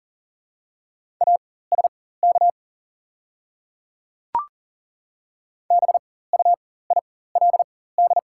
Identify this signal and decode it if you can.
Morse code, heard as a keyed tone, about 40 wpm, spaced as for 19 wpm, 700 Hz. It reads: ASK BUILD
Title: 4x Character Spacing - Top 200 Words - 40wpm